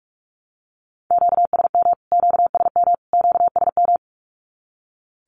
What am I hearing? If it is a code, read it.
Morse code, heard as a keyed tone, about 44 wpm, 700 Hz. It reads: QSM QSM QSM
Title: Q-Codes Repeated 3x 45wpm